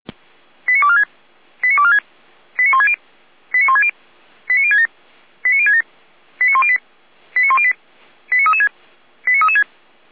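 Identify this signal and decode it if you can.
Five-tone selective call (ZVEI), heard as quick runs of five tones, tones of about 70 ms. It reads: 89246 89246 89160 89160 89067 89067 89108 89108 89336 89336
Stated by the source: ZVEI five-tone call 89072 received on an SDR.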